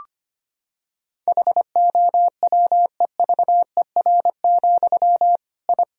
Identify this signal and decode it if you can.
Morse code, heard as a keyed tone, about 25 wpm, 700 Hz. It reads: HOWEVER, I
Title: Sentences from Top 1000 Words - Review Part 1 - 25wpm